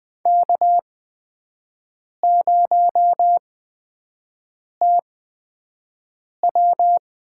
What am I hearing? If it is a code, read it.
Morse code, heard as a keyed tone, about 20 wpm, spaced as for 5 wpm, 700 Hz. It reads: K0TW
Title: USA Call Signs - 8x Character Spacing - 20wpm